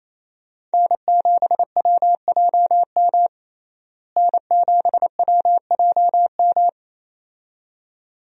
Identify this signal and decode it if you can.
Morse code, heard as a keyed tone, about 27 wpm, 700 Hz. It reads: N7WJM N7WJM
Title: USA Call Signs - 5 Characters - Repeated 2x - 28wpm